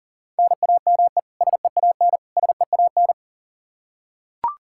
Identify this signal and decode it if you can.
Morse code, heard as a keyed tone, about 40 wpm, 700 Hz. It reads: NAME SEAN SEAN
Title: Practice QSO - Exchange 1 - Beginning - 40wpm